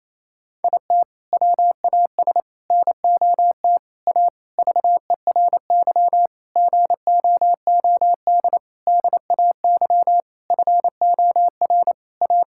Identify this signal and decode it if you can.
Morse code, heard as a keyed tone, about 27 wpm, 700 Hz. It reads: IT WAS NOT A VERY GOOD DAY FOR A
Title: Sentences from Top 1000 Words - Review Part 6 - 28wpm